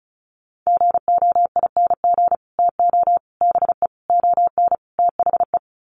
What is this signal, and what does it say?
Morse code, heard as a keyed tone, about 34 wpm, 700 Hz. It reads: GOING TO BE ON THE
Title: Sets of 5 Words Encore2 35wpm